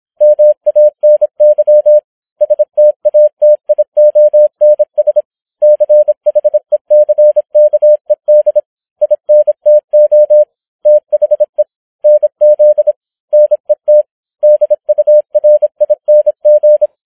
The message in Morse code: MANY STATIONS CHECKED INTO THE NZ NET DURING